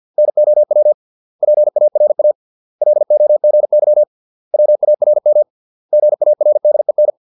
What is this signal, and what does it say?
Morse code, heard as a keyed tone, about 50 wpm, 600 Hz. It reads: NOW PARA LOGY WARM GARDEN